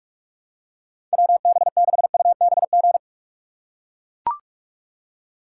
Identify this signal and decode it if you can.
Morse code, heard as a keyed tone, about 41 wpm, 700 Hz. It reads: WB6UBG
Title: USA Call Signs - 6 Characters - 8x Character Spacing - 45wpm